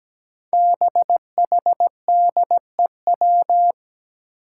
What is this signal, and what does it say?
Morse code, heard as a keyed tone, about 17 wpm, 700 Hz. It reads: BHDEW
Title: Five Challenging Letter-Number Combinations 17wpm